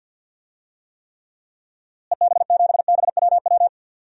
Morse code, heard as a keyed tone, about 51 wpm, 700 Hz. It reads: EB7BWW
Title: DXCC-100 Call Signs - 6 Characters - Repeated 3x - 50wpm